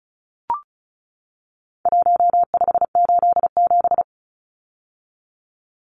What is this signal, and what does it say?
Morse code, heard as a keyed tone, about 34 wpm, 700 Hz. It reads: 1587
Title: Numbers - Contest Style - 4-Digit 35wpm